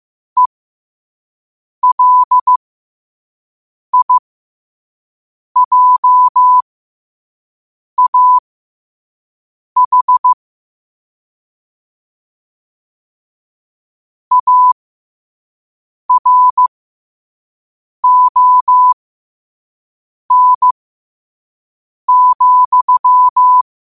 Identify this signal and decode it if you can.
Morse code, heard as a keyed tone, about 15 wpm, spaced as for 5 wpm, 1000 Hz. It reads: ELIJAH ARON,
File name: qotdfiles / 5wpm